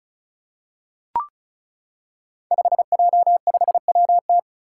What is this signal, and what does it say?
Morse code, heard as a keyed tone, about 36 wpm, 700 Hz. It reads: 5J5WT
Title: Five Challenging Letter-Number Combinations 35wpm